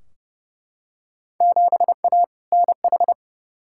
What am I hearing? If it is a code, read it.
Morse code, heard as a keyed tone, about 30 wpm, 700 Hz. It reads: 7A NH
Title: Field Day Exchanges - 30wpm